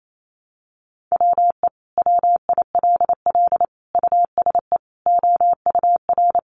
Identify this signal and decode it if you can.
Morse code, heard as a keyed tone, about 28 wpm, 700 Hz. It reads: WE WILL USE OUR
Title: Sentences from Top 1100 Words 28wpm